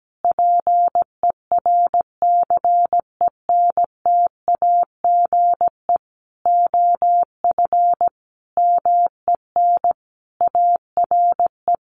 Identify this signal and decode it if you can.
Morse code, heard as a keyed tone, about 17 wpm, 700 Hz. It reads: PERCENTAGE OF MEN ARE
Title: Sentences from Top 1300 Words 17wpm